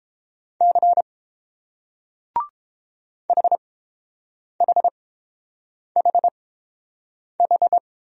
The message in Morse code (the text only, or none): C HHHH